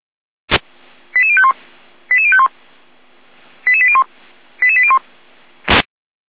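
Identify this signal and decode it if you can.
Five-tone selective call (ZVEI), heard as quick runs of five tones, tones of about 70 ms. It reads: 88051 88051 88801 88801
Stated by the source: ZVEI five-tone call received on an SDR.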